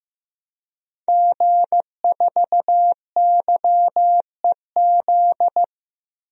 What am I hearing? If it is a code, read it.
Morse code, heard as a keyed tone, about 15 wpm, 700 Hz. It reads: G4YEZ
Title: DXCC-100 Call Signs - 5 Characters - Repeated 3x - 15wpm